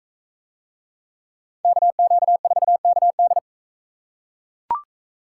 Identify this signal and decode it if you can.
Morse code, heard as a keyed tone, about 40 wpm, 700 Hz. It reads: KQ4KD